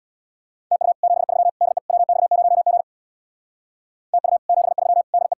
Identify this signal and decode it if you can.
Morse code, heard as a keyed tone, about 57 wpm, 700 Hz. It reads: TU 73 DE WX0G TU 73 DE
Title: Speed-Racing Common QSO Phrases - 50wpm